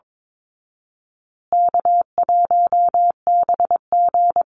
Morse code, heard as a keyed tone, about 22 wpm, 700 Hz. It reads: K1BG
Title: Easy Call Signs Repeated 3x - 22wpm